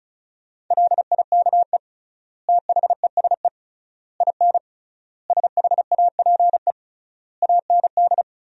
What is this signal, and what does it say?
Morse code, heard as a keyed tone, about 34 wpm, 700 Hz. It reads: LIKE THESE IN SHAPE AND
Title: 3x Word-Spacing - Sentences from Top 500 Words - 35wpm